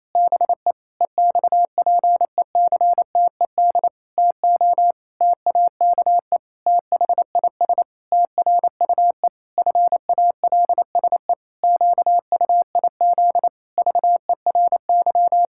Morse code, heard as a keyed tone, about 28 wpm, 700 Hz. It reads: BE EXPECTED TO TAKE THIS TRUE FALSE QUIZ VERY
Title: Pangram Challenge v2 - 28wpm